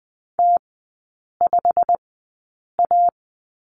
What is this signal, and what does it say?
Morse code, heard as a keyed tone, about 20 wpm, spaced as for 8 wpm, 690 Hz. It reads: T5A
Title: 087 Lesson 15e - Number 5 - Three Characters 20wpm